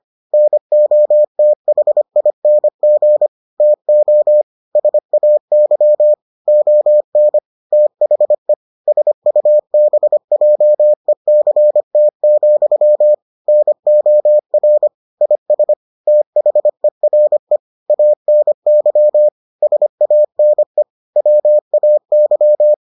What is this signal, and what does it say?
Morse code, heard as a keyed tone, about 25 wpm, 600 Hz. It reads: NOTHING TO SAY ON THE SUBJECT, NOR IS THERE ANY SANE WAY